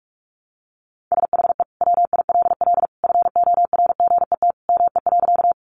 Morse code, heard as a keyed tone, about 44 wpm, 700 Hz. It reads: SHE WILL FORGET ME.
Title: Sentences from Top 1200 Words 45wpm